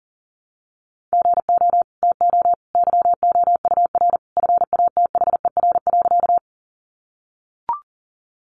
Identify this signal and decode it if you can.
Morse code, heard as a keyed tone, about 40 wpm, 700 Hz. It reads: GO TO YOUR FATHER.
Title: Sentences from Top 300 Words 40wpm